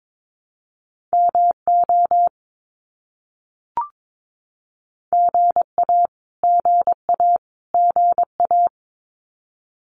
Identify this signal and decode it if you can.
Morse code, heard as a keyed tone, about 22 wpm, 700 Hz. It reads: MO GA GA GA
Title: US States Repeated 3x 22wpm